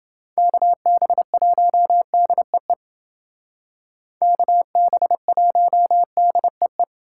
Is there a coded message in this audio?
Yes: Morse code from 0.4 to 6.8 s, heard as a keyed tone, about 29 wpm, 700 Hz. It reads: KB1DEE KB1DEE